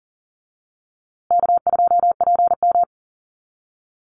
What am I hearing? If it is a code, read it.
Morse code, heard as a keyed tone, about 40 wpm, 700 Hz. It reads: K2PM
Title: USA Call Signs Repeated 3x 40wpm